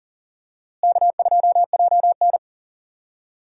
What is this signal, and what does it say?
Morse code, heard as a keyed tone, about 40 wpm, 700 Hz. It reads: K2JN
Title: USA Call Signs - 4 Characters - Repeated 2x - 40wpm